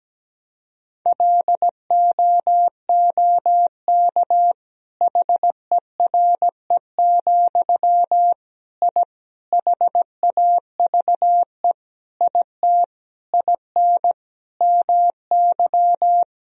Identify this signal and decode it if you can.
Morse code, heard as a keyed tone, about 17 wpm, 700 Hz. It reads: LOOK HERE, I HAVE IT IN MY
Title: Sentences from Top 1000 Words - Review Part 1 - 17wpm